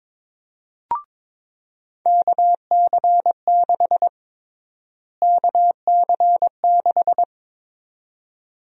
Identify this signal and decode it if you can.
Morse code, heard as a keyed tone, about 22 wpm, 700 Hz. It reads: KC6 KC6
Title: USA Call Sign Prefixes - Repeated 2x - 22wpm